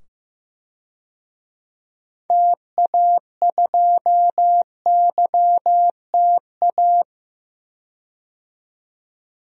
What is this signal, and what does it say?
Morse code, heard as a keyed tone, about 16 wpm, 700 Hz. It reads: TA2YTA
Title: DXCC-100 Call Signs - 6 Characters - 15wpm